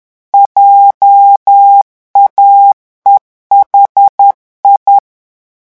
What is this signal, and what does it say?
Morse code, heard as a keyed tone, about 10 wpm, 800 Hz. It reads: JAEHI
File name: morse mp3 / revision11 lettres vitesse 10